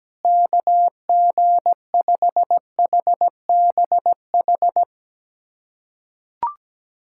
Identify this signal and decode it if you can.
Morse code, heard as a keyed tone, about 17 wpm, 700 Hz. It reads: KG5HBH
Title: USA Callsign DE Callsign - 6 Characters - 17wpm